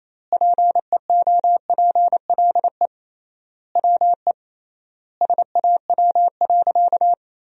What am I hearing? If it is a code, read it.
Morse code, heard as a keyed tone, about 27 wpm, 700 Hz. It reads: PEOPLE WE SAW.
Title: 3x Word-Spacing - Sentences from Top 300 Words - 28wpm